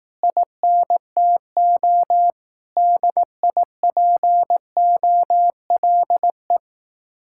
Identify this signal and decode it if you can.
Morse code, heard as a keyed tone, about 18 wpm, 700 Hz. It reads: INTO DIPOLE
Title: Speed-Racing Common QSO Phrases Encore - 15wpm